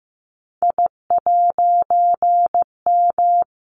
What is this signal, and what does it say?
Morse code, heard as a keyed tone, about 15 wpm, 700 Hz. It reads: I'M
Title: A Sound Of Thunder 15wpm